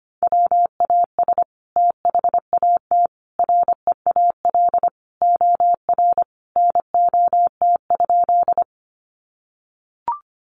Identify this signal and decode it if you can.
Morse code, heard as a keyed tone, about 25 wpm, 700 Hz. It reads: WAS THAT REAL OR NOT?
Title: Sentences from Top 600 Words 25wpm